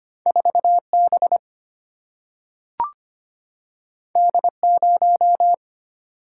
Morse code, heard as a keyed tone, about 25 wpm, 700 Hz. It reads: 4B D0
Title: Single-Letter Single-Number 25wpm